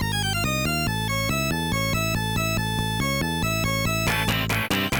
There is no code.